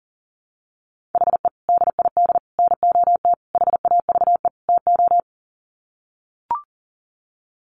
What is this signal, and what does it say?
Morse code, heard as a keyed tone, about 40 wpm, 700 Hz. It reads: HE DID NOT HAVE TO
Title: Sets of 5 Words 40wpm